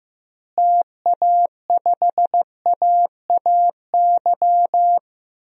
Morse code, heard as a keyed tone, about 15 wpm, 700 Hz. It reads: TA5AAY
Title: DXCC-100 Call Signs - 6 Characters - Repeated 2x - 15wpm